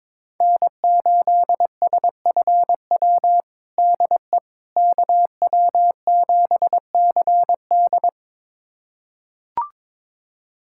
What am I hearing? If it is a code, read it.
Morse code, heard as a keyed tone, about 21 wpm, 700 Hz. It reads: N8SFW DE KW7CD
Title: USA Callsign DE Callsign - 5 Characters - 22wpm